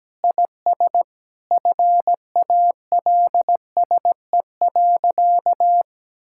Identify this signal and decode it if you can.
Morse code, heard as a keyed tone, about 17 wpm, 700 Hz. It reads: IS FALSE.
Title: Sentences from Top 1600 Words 17wpm